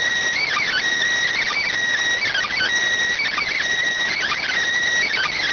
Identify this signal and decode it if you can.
Emergency Alert System (SAME) tones, heard as harsh buzzing bursts. It is heard from 0.4 to 4.5 s.